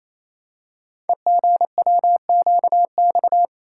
Morse code, heard as a keyed tone, about 28 wpm, 700 Hz. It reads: EGWQX
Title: Code Groups - 5 Letters - 28wpm